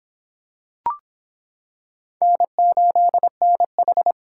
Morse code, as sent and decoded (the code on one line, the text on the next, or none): -. ---.. -. ....
N8NH